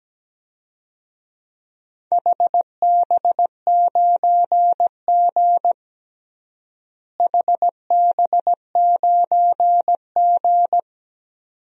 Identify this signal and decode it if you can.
Morse code, heard as a keyed tone, about 17 wpm, 700 Hz. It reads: HB9G HB9G
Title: DXCC-100 Call Signs - 4 Characters - Repeated 2x - 17wpm